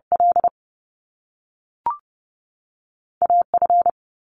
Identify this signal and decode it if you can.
Morse code, heard as a keyed tone, about 30 wpm, 700 Hz. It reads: L A F
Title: Top Two Letter Combinations in Words 30wpm